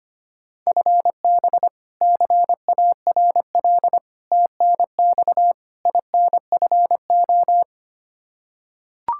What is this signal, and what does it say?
Morse code, heard as a keyed tone, about 25 wpm, 700 Hz. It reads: FB CARL TNX INFO
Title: Practice QSO - Exchange 3 - More Info - 25wpm